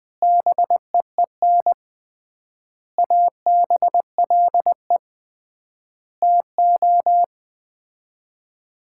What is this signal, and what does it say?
Morse code, heard as a keyed tone, about 21 wpm, 700 Hz. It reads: BEEN ABLE TO
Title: 3x Word-Spacing - Sets of 5 Words - 20wpm